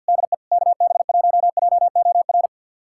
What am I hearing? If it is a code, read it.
Morse code, heard as a keyed tone, about 53 wpm, 700 Hz. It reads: DE KB1JOR
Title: USA Callsign DE Callsign - 6 Characters - 50wpm